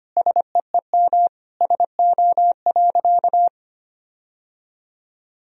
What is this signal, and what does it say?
Morse code, heard as a keyed tone, about 25 wpm, 700 Hz. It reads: SEEM SO.
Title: Sentences from Top 1000 Words - Review Part 1 - 25wpm